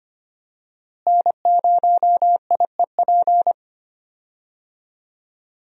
Morse code, heard as a keyed tone, about 25 wpm, 700 Hz. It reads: N0IEP
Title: USA Call Signs - 5 Characters - Repeated 3x - 25wpm